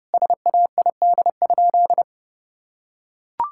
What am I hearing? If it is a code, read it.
Morse code, heard as a keyed tone, about 30 wpm, 700 Hz. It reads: SAID?
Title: Sentences from Top 1600 Words 30wpm